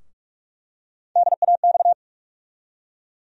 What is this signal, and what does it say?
Morse code, heard as a keyed tone, about 46 wpm, 700 Hz. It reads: DAX
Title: USA Call Sign Postfixes - 3 Letters 45wpm